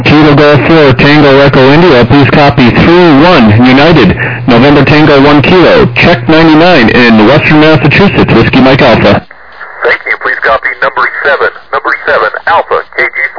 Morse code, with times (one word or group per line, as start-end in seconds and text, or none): none